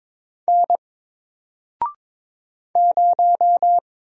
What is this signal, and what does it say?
Morse code, heard as a keyed tone, about 22 wpm, 700 Hz. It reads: N 0
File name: Single-Letter-Number-22wpm